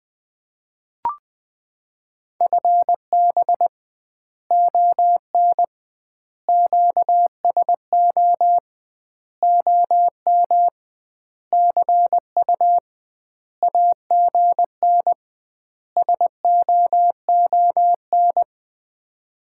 Morse code, heard as a keyed tone, about 20 wpm, 700 Hz. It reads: FB ON QSO OM CU AGN SOON
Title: Common QSO Phrases - 2x Word-Spacing - 20wpm